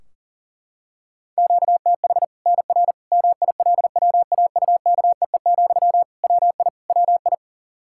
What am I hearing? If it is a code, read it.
Morse code, heard as a keyed tone, about 40 wpm, 700 Hz. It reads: QTH NR MILWAUKEE, WI WI
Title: K1USN SST QSO Practice 40wpm